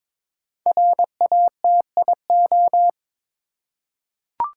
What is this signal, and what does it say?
Morse code, heard as a keyed tone, about 22 wpm, 700 Hz. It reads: RATIO
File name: 5 Letter Words - First 3 Letters Given - 22wpm